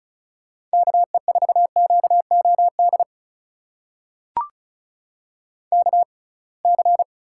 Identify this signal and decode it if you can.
Morse code, heard as a keyed tone, about 36 wpm, spaced as for 27 wpm, 700 Hz. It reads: KE4QOD KC
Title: USA Call Signs - 6 Characters - 6x Character Spacing - 35wpm